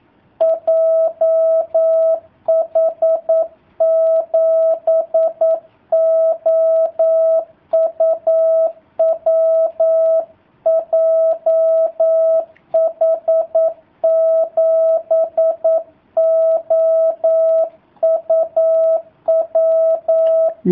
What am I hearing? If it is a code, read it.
Morse code, heard as a keyed tone, about 9 wpm, 650 Hz. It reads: JH7OUW JH7OUW
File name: jh8cbh / 2019cw / sp09- w